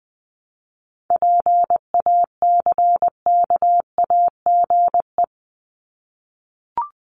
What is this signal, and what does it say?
Morse code, heard as a keyed tone, about 20 wpm, 700 Hz. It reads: PACKAGE
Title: Top 1501-1750 Words 20wpm